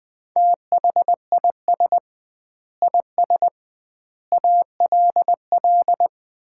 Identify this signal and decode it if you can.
Morse code, heard as a keyed tone, about 20 wpm, 700 Hz. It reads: THIS IS ALL
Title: 2x Word-Spacing - Sets of 5 Words - 20wpm